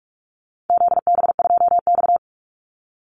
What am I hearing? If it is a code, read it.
Morse code, heard as a keyed tone, about 44 wpm, 700 Hz. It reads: ZB2X
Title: Call Signs Repeated 2x - 45wpm